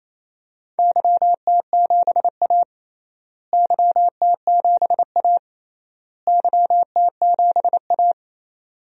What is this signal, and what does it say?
Morse code, heard as a keyed tone, about 28 wpm, 700 Hz. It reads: YT7A YT7A YT7A
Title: DXCC-100 Call Signs - 4 Characters - Repeated 3x - 28wpm